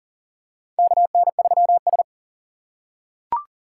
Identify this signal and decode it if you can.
Morse code, heard as a keyed tone, about 40 wpm, 700 Hz. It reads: KN3S
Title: USA Call Signs - 2x Character Spacing - 40wpm